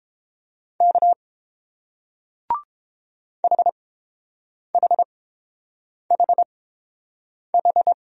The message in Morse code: K HHHH

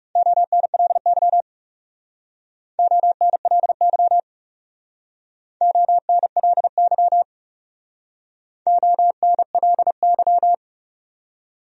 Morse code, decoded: ONLY ONLY ONLY ONLY

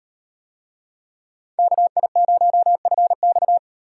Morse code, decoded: KI0FX

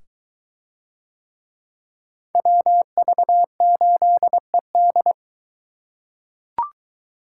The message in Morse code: WV8ED

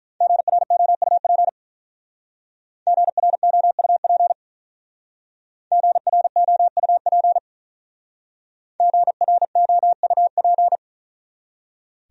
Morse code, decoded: GROUP GROUP GROUP GROUP

